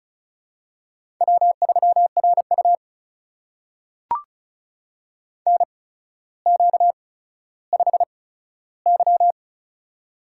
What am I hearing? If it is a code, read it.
Morse code, heard as a keyed tone, about 36 wpm, spaced as for 15 wpm, 700 Hz. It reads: W3RU NQ5Y
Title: USA Call Signs - 4 Characters - 8x Character Spacing - 35wpm